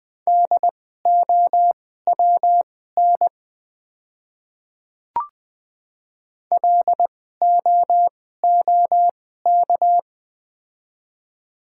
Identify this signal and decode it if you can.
Morse code, heard as a keyed tone, about 20 wpm, spaced as for 15 wpm, 700 Hz. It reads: DOWN LOOK
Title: 2x Character Spacing - Top 100 Words - 20wpm